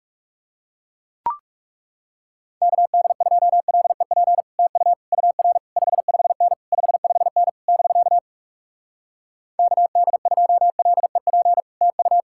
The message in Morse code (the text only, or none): KD2LEP TU UR 55N 55N <BK> KD2LEP TU